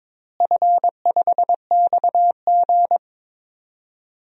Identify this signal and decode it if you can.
Morse code, heard as a keyed tone, about 21 wpm, 700 Hz. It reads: F5XG
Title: DXCC-100 Call Signs - 4 Characters - Repeated 3x - 22wpm